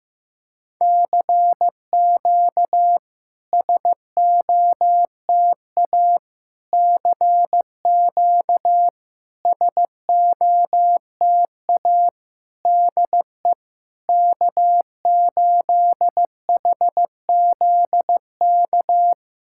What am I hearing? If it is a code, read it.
Morse code, heard as a keyed tone, about 15 wpm, 700 Hz. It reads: CQ SOTA CQ SOTA DE K8HZK
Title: SOTA QSO Practice 15wpm